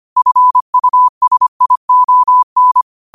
Morse code, heard as a keyed tone, about 25 wpm, 1000 Hz. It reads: FUSION
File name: qotdfiles / 25wpm